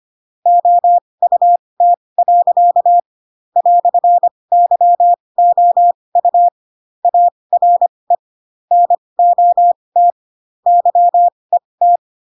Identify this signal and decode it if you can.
Morse code, heard as a keyed tone, about 25 wpm, spaced as for 20 wpm, 700 Hz. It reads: OUT. "YOU ARE NOT YET